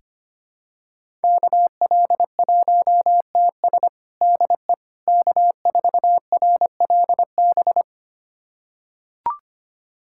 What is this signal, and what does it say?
Morse code, heard as a keyed tone, about 25 wpm, 700 Hz. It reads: KL1TS DE K4RLB
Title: USA Callsign DE Callsign 25wpm